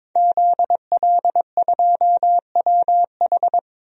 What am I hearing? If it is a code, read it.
Morse code, heard as a keyed tone, about 22 wpm, 700 Hz. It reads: ZL2WH
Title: DXCC-100 Call Signs - 5 Characters - 22wpm